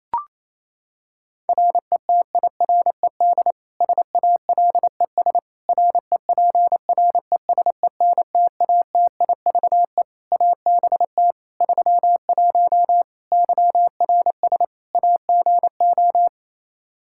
Morse code, heard as a keyed tone, about 28 wpm, 700 Hz. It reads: RETIRED SALES REPRESENTATIVE ABT 31 YRS AGO
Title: Practice QSO - Exchange 3 - More Info - 28wpm